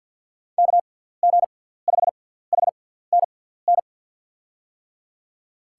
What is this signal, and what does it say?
Morse code, heard as a keyed tone, about 51 wpm, spaced as for 17 wpm, 700 Hz. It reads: KG5HNN
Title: USA Call Signs - 6x Character Spacing - 50wpm